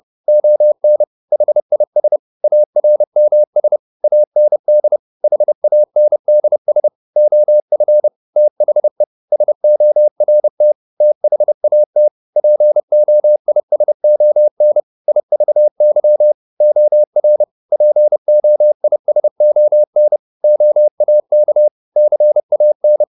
Morse code, heard as a keyed tone, about 30 wpm, 600 Hz. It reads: ON HIS ARMS AND HANDS OF THE SORT THAT POISON IVY OR POISON OAK CAN
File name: the old man and the sea / the old man and the sea 1 30wpm